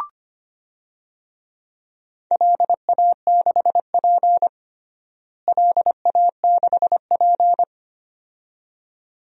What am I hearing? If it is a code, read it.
Morse code, heard as a keyed tone, about 25 wpm, 700 Hz. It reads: LA6P LA6P
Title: DXCC-100 Call Signs - 4 Characters - Repeated 2x - 25wpm